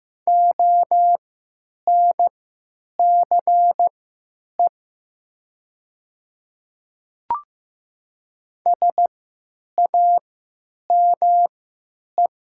Morse code, heard as a keyed tone, about 15 wpm, spaced as for 9 wpm, 700 Hz. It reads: ONCE SAME
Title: 3x Character Spacing - Top 500 Words - 15wpm